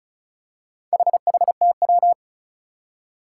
Morse code, heard as a keyed tone, about 36 wpm, 700 Hz. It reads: HHTW
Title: Four Challenging Letter-Number Combinations 35wpm